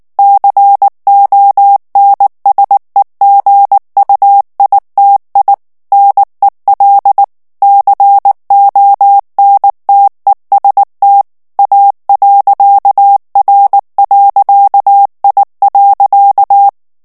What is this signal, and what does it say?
Morse code, heard as a keyed tone, about 19 wpm, 800 Hz. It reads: CONSEGUITI NEL CONTEST A.R.I.
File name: morse mp3 / ITALIEN texte clair 2 vitesse 18 mots minute